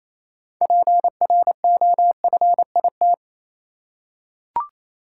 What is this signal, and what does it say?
Morse code, heard as a keyed tone, about 28 wpm, 700 Hz. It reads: PROFIT
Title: Six Letter Words - 28wpm